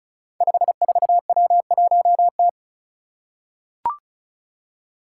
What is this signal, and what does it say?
Morse code, heard as a keyed tone, about 36 wpm, 700 Hz. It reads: 54W1T
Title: Five Challenging Letter-Number Combinations 35wpm